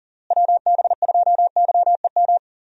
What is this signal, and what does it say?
Morse code, heard as a keyed tone, about 40 wpm, 700 Hz. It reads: WB2YEM
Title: USA Call Signs - 6 Characters - Repeated 3x - 40wpm